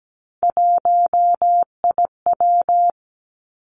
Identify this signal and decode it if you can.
Morse code, heard as a keyed tone, about 17 wpm, 700 Hz. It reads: 1IW
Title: Three Challenging Letter-Number Combinations 17wpm